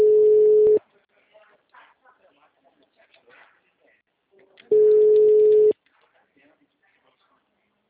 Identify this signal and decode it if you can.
Ringback tone (Europe, 425 Hz), with speech.